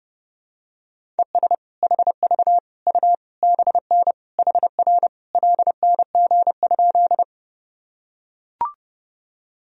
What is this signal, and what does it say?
Morse code, heard as a keyed tone, about 32 wpm, 700 Hz. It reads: ES HV U BN HR LNG?
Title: Sentences from Top 200 Words and Abbreviations 30wpm